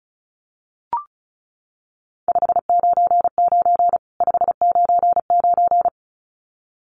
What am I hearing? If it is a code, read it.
Morse code, heard as a keyed tone, about 34 wpm, 700 Hz. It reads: 599 599
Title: Common QSO Phrases 35wpm